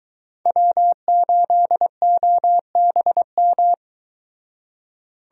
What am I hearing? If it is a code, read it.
Morse code, heard as a keyed tone, about 22 wpm, 700 Hz. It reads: W8OBM